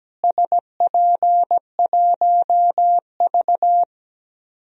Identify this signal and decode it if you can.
Morse code, heard as a keyed tone, about 17 wpm, 700 Hz. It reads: SP1V
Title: DXCC-100 Call Signs - 4 Characters - Repeated 3x - 17wpm